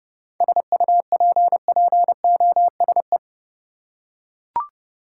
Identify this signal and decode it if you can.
Morse code, heard as a keyed tone, about 30 wpm, 700 Hz. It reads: SUPPOSE
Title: Seven Letter Words - 30wpm